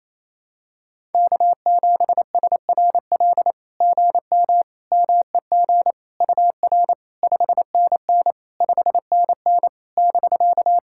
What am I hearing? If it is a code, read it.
Morse code, heard as a keyed tone, about 28 wpm, 700 Hz. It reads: K7SRL GM MEG UR 5NN 5NN <BK>
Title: SOTA QSO Practice 28wpm